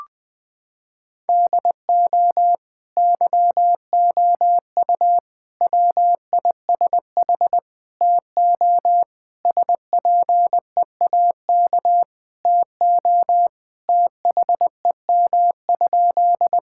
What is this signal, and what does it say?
Morse code, heard as a keyed tone, about 20 wpm, 700 Hz. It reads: DO YOU WISH TO SPEAK TO THEM?
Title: Sentences from Top 1600 Words 20wpm